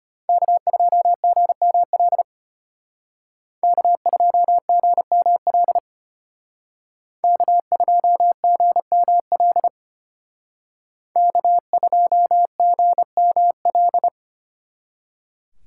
Morse code, heard as a keyed tone, about 32 wpm, 700 Hz. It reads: K2GML K2GML K2GML K2GML